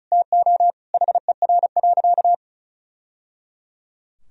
Morse code, heard as a keyed tone, about 36 wpm, 700 Hz. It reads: TO HER.
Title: Sentences from Top 1000 Words - Review Part 4 - 35wpm